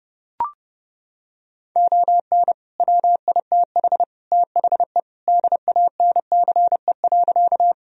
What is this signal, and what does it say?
Morse code, heard as a keyed tone, about 30 wpm, 700 Hz. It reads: ON WITH THE DANCE.
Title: Sentences from Top 1000 Words - Review Part 4 - 30wpm